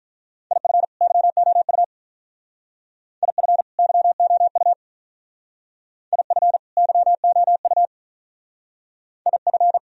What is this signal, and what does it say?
Morse code, heard as a keyed tone, about 45 wpm, 700 Hz. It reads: IF YOU IF YOU IF YOU IF